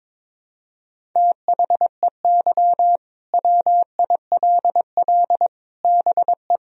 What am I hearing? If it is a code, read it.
Morse code, heard as a keyed tone, about 22 wpm, 700 Hz. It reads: THEY WILL BE
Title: Sentences from Top 1000 Words - Review Part 1 - 22wpm